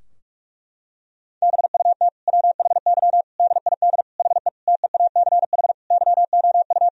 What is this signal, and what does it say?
Morse code, heard as a keyed tone, about 44 wpm, 700 Hz. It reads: BUT WHY DID HE TEACH YOU
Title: Sentences from Top 1000 Words - Review Part 2 - 45wpm